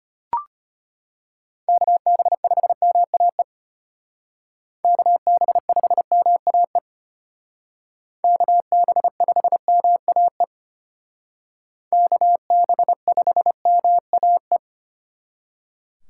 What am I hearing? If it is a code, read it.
Morse code, heard as a keyed tone, about 30 wpm, 700 Hz. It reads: KB5MAE KB5MAE KB5MAE KB5MAE
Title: Speed Racing - USA 2x3 Call Signs 25wpm